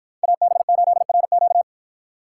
Morse code, heard as a keyed tone, about 53 wpm, 700 Hz. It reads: AB8RQ